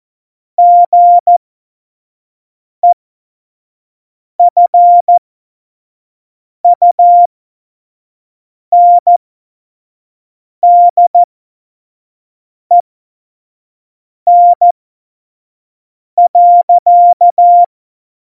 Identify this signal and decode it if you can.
Morse code, heard as a keyed tone, about 14 wpm, spaced as for 5 wpm, 700 Hz. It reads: GEFUNDEN.
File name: Morsekurs / Witze / Witz 29. Januar 2019, 25 F70